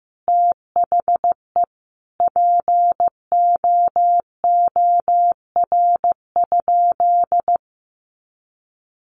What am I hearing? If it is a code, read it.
Morse code, heard as a keyed tone, about 15 wpm, 700 Hz. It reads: THE POOR?
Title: Sentences from Top 1100 Words 15wpm